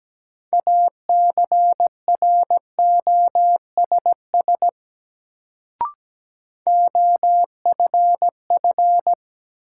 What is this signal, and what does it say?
Morse code, heard as a keyed tone, about 17 wpm, 700 Hz. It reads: ACROSS OFF
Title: Sets of 2 Words - New Word Drill - 17wpm